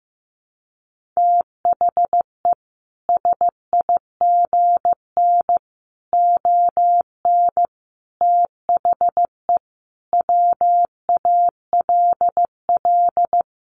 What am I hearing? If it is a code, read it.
Morse code, heard as a keyed tone, about 15 wpm, 700 Hz. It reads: THE SIGN ON THE WALL
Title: A Sound Of Thunder 15wpm